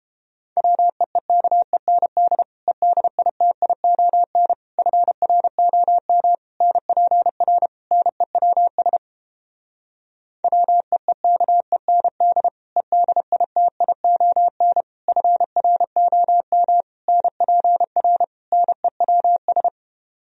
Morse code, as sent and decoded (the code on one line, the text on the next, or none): .-- . -.- . -. -.. / . -.. .. - .. --- -. / ..-. .-. --- -- / -. .--. .-. / -. .-- ... / .-- . -.- . -. -.. / . -.. .. - .. --- -. / ..-. .-. --- -- / -. .--. .-. / -. .-- ...
WEEKEND EDITION FROM NPR NEWS WEEKEND EDITION FROM NPR NEWS